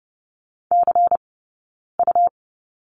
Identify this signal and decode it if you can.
Morse code, heard as a keyed tone, about 30 wpm, spaced as for 9 wpm, 690 Hz. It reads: CU
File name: Lesson 17d - Letter u - Two Characters 30wpm